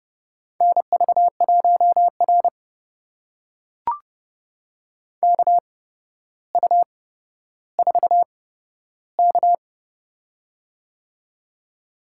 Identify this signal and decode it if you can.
Morse code, heard as a keyed tone, about 30 wpm, spaced as for 13 wpm, 700 Hz. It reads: NV1R KU4K
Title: USA Call Signs - 4 Characters - 8x Character Spacing - 30wpm